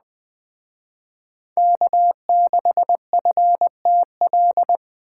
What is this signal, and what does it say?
Morse code, heard as a keyed tone, about 20 wpm, 700 Hz. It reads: K6FTL
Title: Speed Racing - USA 1x3 Call Signs 17wpm